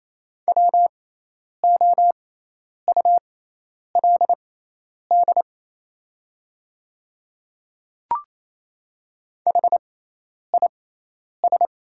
Morse code, heard as a keyed tone, about 28 wpm, spaced as for 10 wpm, 690 Hz. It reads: WOULD HIS
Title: 6x Character Spacing - Top 300 Words - 28wpm